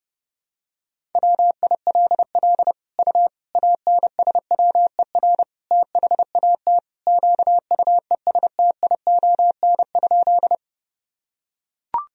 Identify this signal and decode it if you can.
Morse code, heard as a keyed tone, about 30 wpm, 700 Hz. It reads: WILL U ANSWER THAT QUESTION?
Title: Sentences from Top 400 Words and Abbreviations 30wpm